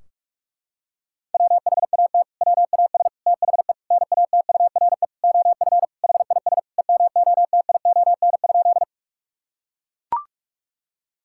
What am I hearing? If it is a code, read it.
Morse code, heard as a keyed tone, about 42 wpm, 700 Hz. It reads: WHAT WAS THE NATURE OF HIS EMOTION?
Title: Sentences from Top 1600 Words 45wpm